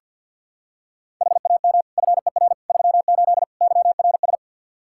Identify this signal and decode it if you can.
Morse code, heard as a keyed tone, about 52 wpm, 700 Hz. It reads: HAM FER 38 YRS
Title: K1USN SST QSO Practice 50wpm